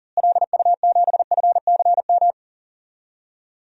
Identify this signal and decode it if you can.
Morse code, heard as a keyed tone, about 40 wpm, 700 Hz. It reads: LU7FCM
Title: DXCC-100 Call Signs - 6 Characters - Repeated 2x - 40wpm